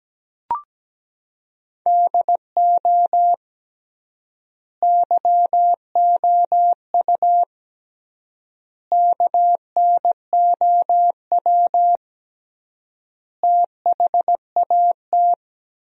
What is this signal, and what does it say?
Morse code, heard as a keyed tone, about 17 wpm, 700 Hz. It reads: DO YOU KNOW THAT
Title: 3x Word-Spacing - Sentences from Top 400 Words - 17wpm